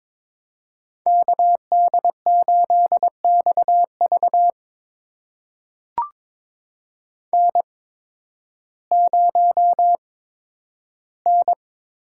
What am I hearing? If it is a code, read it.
Morse code, heard as a keyed tone, about 22 wpm, spaced as for 12 wpm, 700 Hz. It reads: KD8XV N0N
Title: USA Call Signs - 5 Characters - 8x Character Spacing - 22wpm